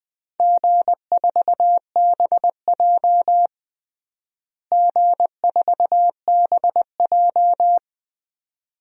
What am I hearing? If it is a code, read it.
Morse code, heard as a keyed tone, about 20 wpm, 700 Hz. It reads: G4BJ G4BJ
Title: DXCC-100 Call Signs - 4 Characters - Repeated 2x - 20wpm